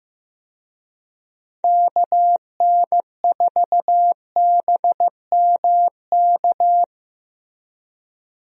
Morse code, heard as a keyed tone, about 15 wpm, 700 Hz. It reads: KN4BMK
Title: USA Call Signs - 6 Characters - Repeated 3x - 15wpm